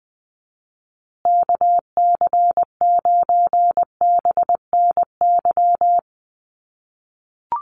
Morse code, heard as a keyed tone, about 20 wpm, 700 Hz. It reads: KC9BNY
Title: USA Call Signs Repeated 2x 20wpm